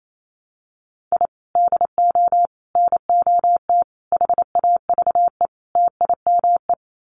Morse code, heard as a keyed tone, about 28 wpm, 700 Hz. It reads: I DO NOT HAVE TIME
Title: Sets of 5 Words Encore 28wpm